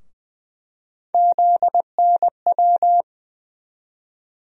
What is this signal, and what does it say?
Morse code, heard as a keyed tone, about 20 wpm, 700 Hz. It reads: ZNW
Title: USA Call Sign Postfixes - 3 Letters 20wpm